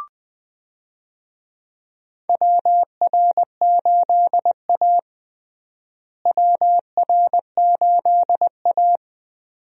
Morse code, heard as a keyed tone, about 20 wpm, 700 Hz. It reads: WR8A WR8A
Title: USA Call Signs - 4 Characters - Repeated 2x - 20wpm